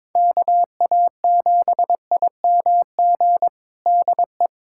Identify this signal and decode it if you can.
Morse code, heard as a keyed tone, about 22 wpm, 700 Hz. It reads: KA7IMG DE
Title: USA Callsign DE Callsign - 6 Characters - 22wpm